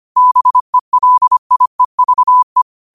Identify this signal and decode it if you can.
Morse code, heard as a keyed tone, about 25 wpm, 1000 Hz. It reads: BELIEVE
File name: qotdfiles / 25wpm